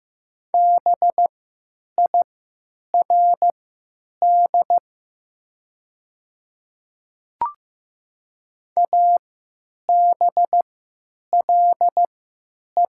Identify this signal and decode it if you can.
Morse code, heard as a keyed tone, about 15 wpm, spaced as for 8 wpm, 700 Hz. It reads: BIRD ABLE
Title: 3x Character Spacing - Top 500 Words - 15wpm